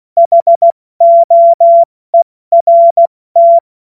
Morse code, heard as a keyed tone, about 16 wpm, 680 Hz. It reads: HOERT
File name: Morsekurs / Witze / Witz 20. September 2022 70 F80